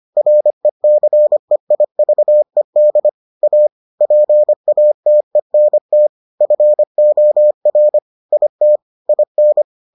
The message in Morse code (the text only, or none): RECEIVED A PATENT FOR IT IN